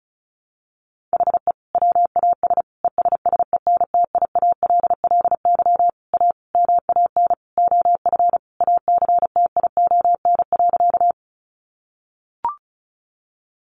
Morse code, heard as a keyed tone, about 34 wpm, 700 Hz. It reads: HE WAS ESSENTIALLY A MAN OF ACTION.
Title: Sentences from Top 1300 Words 35wpm